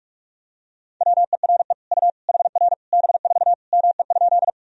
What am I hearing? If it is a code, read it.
Morse code, heard as a keyed tone, about 47 wpm, 700 Hz. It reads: WERE U HR B4 ME?
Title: Sentences from Top 200 Words and Abbreviations 45wpm